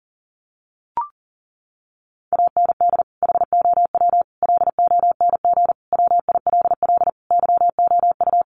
Morse code, heard as a keyed tone, about 40 wpm, 700 Hz. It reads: AND HOW LONG WILL YOU
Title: Sentences from Top 1200 Words 40wpm